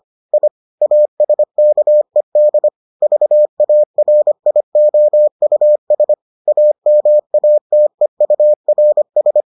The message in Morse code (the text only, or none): I ASKED VARIOUS AMATEURS